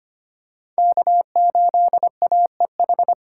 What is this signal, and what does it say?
Morse code, heard as a keyed tone, about 25 wpm, 700 Hz. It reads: K8AEH
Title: USA Call Signs - 5 Characters - 8x Character Spacing - 25wpm